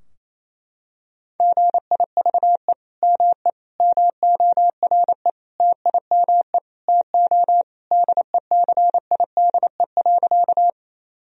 Morse code, heard as a keyed tone, about 28 wpm, 700 Hz. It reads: GIVE ME MORE TIME TO DECIDE.
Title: Sentences from Top 1000 Words - Review Part 3 - 28wpm